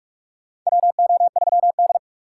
Morse code, heard as a keyed tone, about 44 wpm, 700 Hz. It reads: WO3D
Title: USA Call Signs - 6x Character Spacing - 45wpm